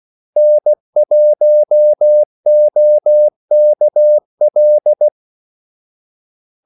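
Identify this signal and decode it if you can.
Morse code, heard as a keyed tone, about 16 wpm, 600 Hz. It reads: N1OKL
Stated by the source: Morse call sign practice at 16 wpm.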